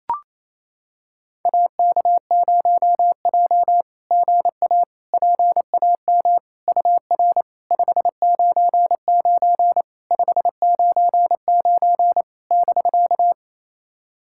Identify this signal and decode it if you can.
Morse code, heard as a keyed tone, about 28 wpm, 700 Hz. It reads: AK0J GA PAM UR 599 599 <BK>
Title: SOTA QSO Practice 28wpm